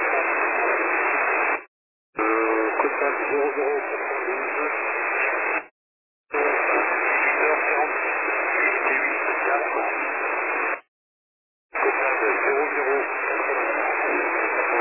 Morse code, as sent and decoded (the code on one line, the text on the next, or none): none